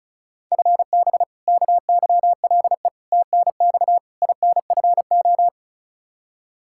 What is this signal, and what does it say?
Morse code, heard as a keyed tone, about 36 wpm, 700 Hz. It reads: FB KYLE TNX INFO
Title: Practice QSO - Exchange 3 - More Info - 35wpm